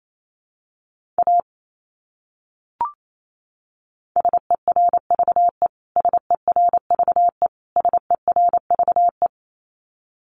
Morse code, heard as a keyed tone, about 28 wpm, 700 Hz. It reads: A SERVE SERVE SERVE
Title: Top 500 Words Repeated 28wpm